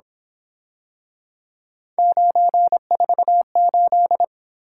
Morse code, heard as a keyed tone, about 26 wpm, 700 Hz. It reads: 948
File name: Speed-Racing - Three Letter-Number Combinations - 22wpm